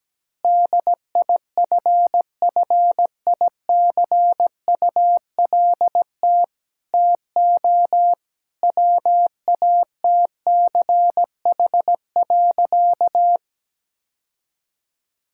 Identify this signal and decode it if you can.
Morse code, heard as a keyed tone, about 17 wpm, 700 Hz. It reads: DIFFICULT TO WATCH.
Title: Sentences from Top 1000 Words - Review Part 5 - 17wpm